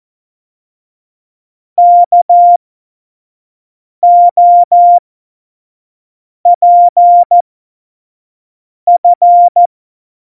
Morse code, heard as a keyed tone, about 14 wpm, spaced as for 5 wpm, 700 Hz. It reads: KOPF
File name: Tempo 25 F70